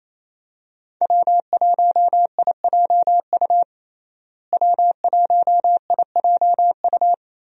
Morse code, heard as a keyed tone, about 28 wpm, 700 Hz. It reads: W1IJU W1IJU
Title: USA Call Signs - 5 Characters - Repeated 3x - 28wpm